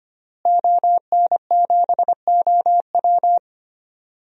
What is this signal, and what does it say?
Morse code, heard as a keyed tone, about 25 wpm, 700 Hz. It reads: ON7OW
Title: DXCC-100 Call Signs - 5 Characters - Repeated 3x - 25wpm